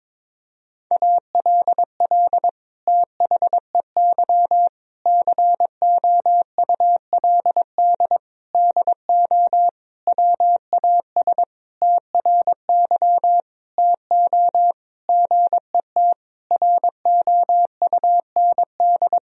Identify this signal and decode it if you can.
Morse code, heard as a keyed tone, about 22 wpm, 700 Hz. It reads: ALL THEY COULD DO WAS TRY TO GET ROUND
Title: Sentences from Top 1000 Words - Review Part 1 - 22wpm